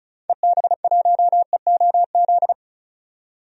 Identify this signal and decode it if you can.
Morse code, heard as a keyed tone, about 34 wpm, 700 Hz. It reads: EB1EOZ
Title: DXCC-100 Call Signs - 6 Characters - 35wpm